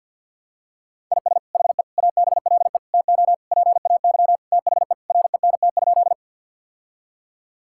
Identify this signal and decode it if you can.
Morse code, heard as a keyed tone, about 50 wpm, 700 Hz. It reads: IS HE ABLE TO PAY THE RENT?
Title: Sentences from Top 1000 Words - Review Part 5 - 50wpm